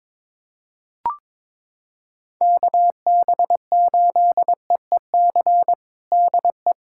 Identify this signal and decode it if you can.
Morse code, heard as a keyed tone, about 22 wpm, 700 Hz. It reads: KB8EEC DE
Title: USA Callsign DE Callsign - 6 Characters - 22wpm